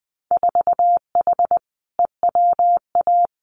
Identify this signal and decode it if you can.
Morse code, heard as a keyed tone, about 20 wpm, 700 Hz. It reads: 4H EWA
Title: Winter Field Day Exchanges Encore 20wpm